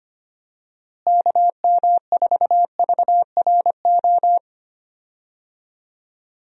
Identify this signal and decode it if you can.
Morse code, heard as a keyed tone, about 25 wpm, 700 Hz. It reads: KM4VRO
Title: USA Call Signs - 6 Characters - Repeated 3x - 25wpm